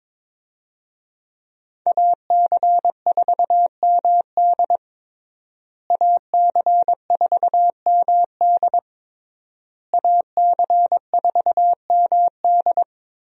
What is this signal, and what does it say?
Morse code, heard as a keyed tone, about 22 wpm, 700 Hz. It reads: AC4MD AC4MD AC4MD
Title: USA Call Signs - 5 Characters - Repeated 3x - 22wpm